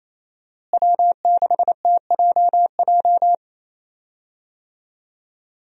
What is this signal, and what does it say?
Morse code, heard as a keyed tone, about 28 wpm, 700 Hz. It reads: W6TJJ
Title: USA Call Signs - 5 Characters - Repeated 3x - 28wpm